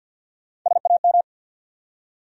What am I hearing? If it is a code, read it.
Morse code, heard as a keyed tone, about 52 wpm, 700 Hz. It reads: SAM
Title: USA Common Names - Max 3 Letters - 50wpm